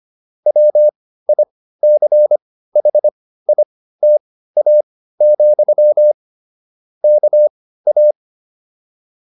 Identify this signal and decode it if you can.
Morse code, heard as a keyed tone, about 25 wpm, spaced as for 15 wpm, 600 Hz. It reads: WICHITA, KA